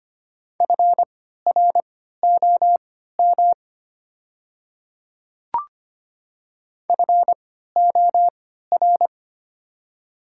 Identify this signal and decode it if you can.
Morse code, heard as a keyed tone, about 25 wpm, spaced as for 14 wpm, 700 Hz. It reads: FROM FOR
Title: 3x Character Spacing - Top 100 Words - 25wpm